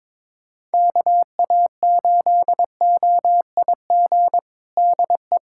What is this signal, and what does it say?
Morse code, heard as a keyed tone, about 22 wpm, 700 Hz. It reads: KA8OIG DE
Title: USA Callsign DE Callsign - 6 Characters - 22wpm